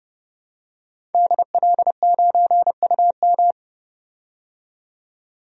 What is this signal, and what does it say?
Morse code, heard as a keyed tone, about 32 wpm, 700 Hz. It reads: DL9UM
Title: DXCC-100 Call Signs - 5 Characters - 30wpm